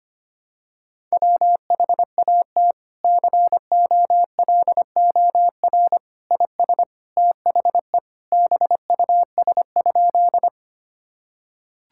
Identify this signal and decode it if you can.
Morse code, heard as a keyed tone, about 25 wpm, 700 Hz. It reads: WHAT COLOR IS THE BUS?
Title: Sentences from Top 1000 Words - Review Part 6 - 25wpm